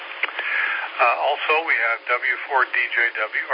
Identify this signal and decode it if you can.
Emergency Alert System (SAME) tones, heard as harsh buzzing bursts, with speech. It is heard from 0.5 to 3.1 s.